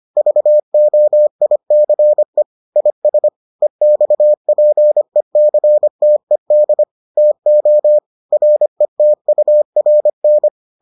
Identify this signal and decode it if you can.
Morse code, heard as a keyed tone, about 25 wpm, 600 Hz. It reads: VOICE IS EXPECTED TO RETURN